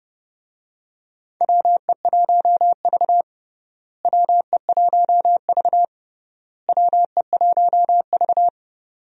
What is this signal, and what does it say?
Morse code, heard as a keyed tone, about 31 wpm, 700 Hz. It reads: WE1V WE1V WE1V
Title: USA Call Signs - 4 Characters - Repeated 3x - 30wpm